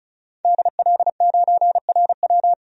Morse code, heard as a keyed tone, about 36 wpm, 700 Hz. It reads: DL9RW
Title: DXCC-100 Call Signs - 5 Characters - 35wpm